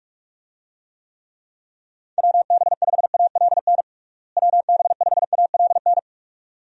Morse code, heard as a keyed tone, about 46 wpm, 700 Hz. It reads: WB5ALN WB5ALN
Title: USA Call Signs - 6 Characters - Repeated 2x - 45wpm